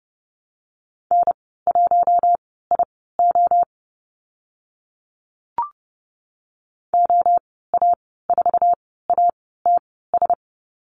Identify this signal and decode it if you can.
Morse code, heard as a keyed tone, about 30 wpm, spaced as for 17 wpm, 690 Hz. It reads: N1IO OA4ATS
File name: Lesson 15g - Number 5 - Call Signs 30wpm